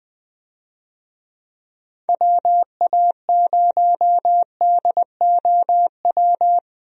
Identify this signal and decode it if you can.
Morse code, heard as a keyed tone, about 21 wpm, 700 Hz. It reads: WA0DOW
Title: USA Call Signs - 6 Characters - Repeated 2x - 20wpm